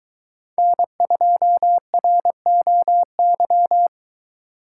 Morse code, heard as a keyed tone, about 22 wpm, 700 Hz. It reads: N2ROY